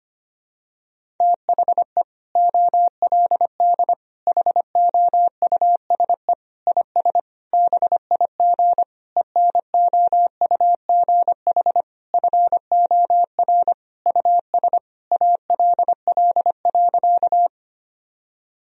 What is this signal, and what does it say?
Morse code, heard as a keyed tone, about 25 wpm, 700 Hz. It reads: THE OLD HOUSE IS BIG ENOUGH FOR US ALL.
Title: Sentences from Top 1000 Words - Review Part 1 - 25wpm